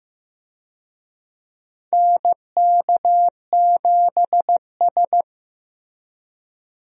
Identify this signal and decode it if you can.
Morse code, heard as a keyed tone, about 15 wpm, 700 Hz. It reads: NK7S
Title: USA Call Signs - 4 Characters - Repeated 3x - 15wpm